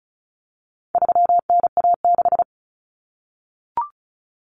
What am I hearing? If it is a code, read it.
Morse code, heard as a keyed tone, about 34 wpm, 700 Hz. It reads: 3NA6
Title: Common Cut Numbers - 4-Digit 35wpm